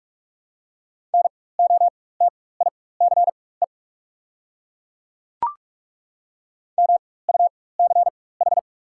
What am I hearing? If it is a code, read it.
Morse code, heard as a keyed tone, about 42 wpm, spaced as for 21 wpm, 700 Hz. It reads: NOTICE MUCH
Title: 4x Character Spacing - Top 500 Words - 45wpm